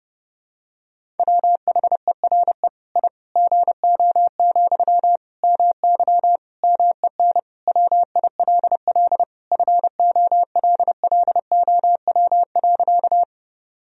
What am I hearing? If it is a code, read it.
Morse code, heard as a keyed tone, about 30 wpm, 700 Hz. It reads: WHERE I GO, MY MEN WILL FOLLOW.
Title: Sentences from Top 1000 Words - Review Part 1 - 30wpm